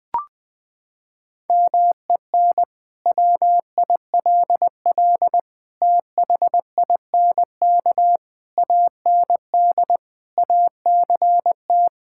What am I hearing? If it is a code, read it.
Morse code, heard as a keyed tone, about 20 wpm, 700 Hz. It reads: MEN WILL THINK AND ACT
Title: Sentences from Top 1000 Words - Review Part 5 - 20wpm